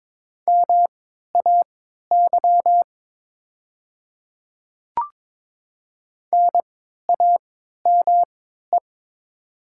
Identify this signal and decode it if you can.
Morse code, heard as a keyed tone, about 22 wpm, spaced as for 12 wpm, 700 Hz. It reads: MAY NAME
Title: 3x Character Spacing - Top 200 Words - 22wpm